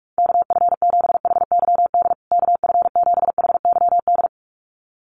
Morse code, heard as a keyed tone, about 44 wpm, 700 Hz. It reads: KF7HYD KF7HYD
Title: USA Call Signs Repeated 2x 45wpm